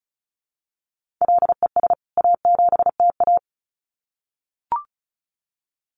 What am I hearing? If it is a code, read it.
Morse code, heard as a keyed tone, about 34 wpm, 700 Hz. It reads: LES A7TA
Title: CWT Exchange 35wpm